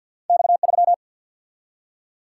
Morse code, heard as a keyed tone, about 53 wpm, 700 Hz. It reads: X3
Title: Single-Letter Single-Number 50wpm